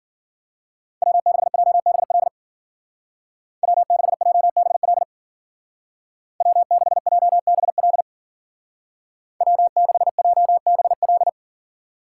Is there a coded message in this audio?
Yes: Morse code from 1.0 to 11.3 s, heard as a keyed tone, about 50 wpm, 700 Hz. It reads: W6JBL W6JBL W6JBL W6JBL